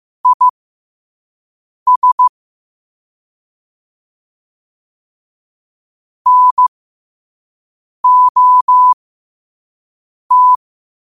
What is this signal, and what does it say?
Morse code, heard as a keyed tone, about 15 wpm, spaced as for 5 wpm, 1000 Hz. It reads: IS NOT
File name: qotdfiles / 5wpm